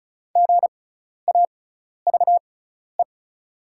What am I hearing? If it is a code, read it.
Morse code, heard as a keyed tone, about 36 wpm, spaced as for 12 wpm, 690 Hz. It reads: GAVE
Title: 6x Character Spacing - Top 500 Words - 35wpm